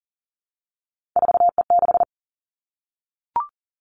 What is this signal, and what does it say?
Morse code, heard as a keyed tone, about 38 wpm, 700 Hz. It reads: 4E6
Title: Three Challenging Letter-Number Combinations 40wpm